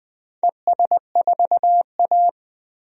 Morse code, heard as a keyed tone, about 21 wpm, 700 Hz. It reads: ES4A
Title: DXCC-100 Call Signs - 4 Characters - 20wpm